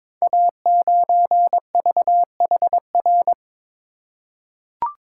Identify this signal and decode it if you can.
Morse code, heard as a keyed tone, about 22 wpm, 700 Hz. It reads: A9VHR
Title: Code Groups - 5 Letter:Numbers - 22wpm